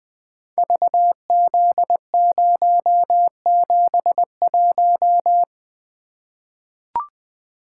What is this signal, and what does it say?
Morse code, heard as a keyed tone, about 21 wpm, 700 Hz. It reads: VZ071
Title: Code Groups - 5 Letter:Numbers - 20wpm